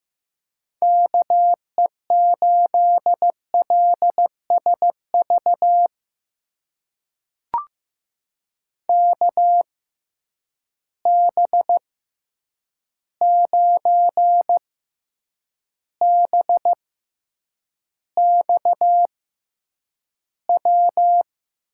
Morse code, heard as a keyed tone, about 15 wpm, spaced as for 8 wpm, 700 Hz. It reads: KE8LSV KB9BXW
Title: USA Call Signs - 6 Characters - 6x Character Spacing - 15wpm